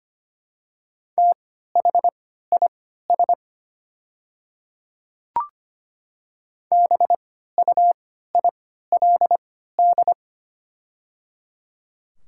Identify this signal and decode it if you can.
Morse code, heard as a keyed tone, about 25 wpm, spaced as for 14 wpm, 700 Hz. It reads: THIS BUILD
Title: 3x Character Spacing - Top 200 Words - 25wpm